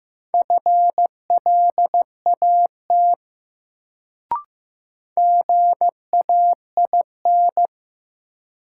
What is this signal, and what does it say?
Morse code, heard as a keyed tone, about 15 wpm, 700 Hz. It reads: FLAT GAIN
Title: Sentences from Top 1000 Words - New Word Drill - 15wpm